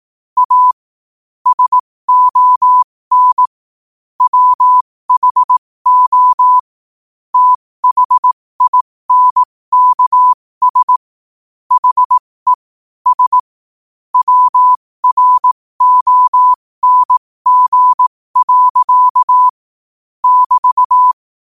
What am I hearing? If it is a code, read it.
Morse code, heard as a keyed tone, about 18 wpm, spaced as for 15 wpm, 1000 Hz. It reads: A SON WHO THINKS HE S WRONG. <BT>
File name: qotdfiles / 15wpm